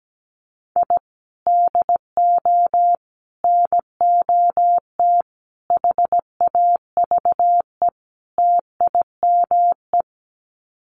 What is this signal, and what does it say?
Morse code, heard as a keyed tone, about 17 wpm, 700 Hz. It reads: I DO NOT HAVE TIME
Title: Sets of 5 Words Encore 17wpm